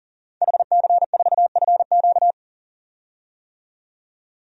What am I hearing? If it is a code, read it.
Morse code, heard as a keyed tone, about 40 wpm, 700 Hz. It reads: HC4FQ
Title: DXCC-100 Call Signs - 5 Characters - 40wpm